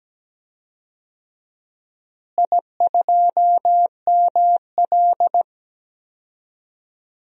Morse code, heard as a keyed tone, about 17 wpm, 700 Hz. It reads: I2ML
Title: DXCC-100 Call Signs - 4 Characters - Repeated 2x - 17wpm